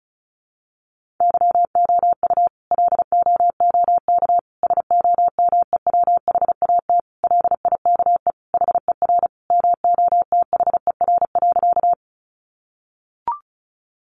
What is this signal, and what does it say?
Morse code, heard as a keyed tone, about 34 wpm, 700 Hz. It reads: YOU LOOK SOMEWHAT LIKE HER MOTHER.
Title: Sentences from Top 1300 Words 35wpm